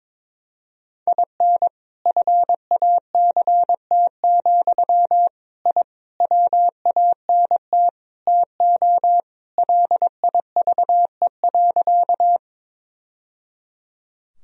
Morse code, heard as a keyed tone, about 22 wpm, 700 Hz. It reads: IN FACT, I WANT TO LIVE.
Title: Sentences from Top 1000 Words - Review Part 3 - 22wpm